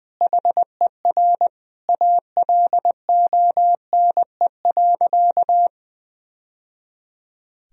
Morse code, heard as a keyed tone, about 20 wpm, 700 Hz. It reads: HER ALONE.
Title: Sentences from Top 1000 Words - Review Part 4 - 20wpm